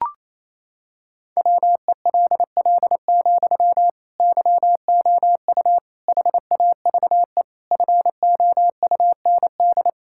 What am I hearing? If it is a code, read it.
Morse code, heard as a keyed tone, about 28 wpm, 690 Hz. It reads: WELL, YOU HAVE FOUND
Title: Sentences from Top 1000 Words - Review Part 6 - 28wpm